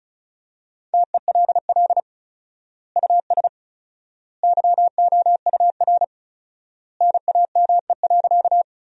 Morse code, heard as a keyed tone, about 36 wpm, 700 Hz. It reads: TELL US YOUR NAME.
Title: 4x Word-Spacing - Sentences from Top 200 Words - 35wpm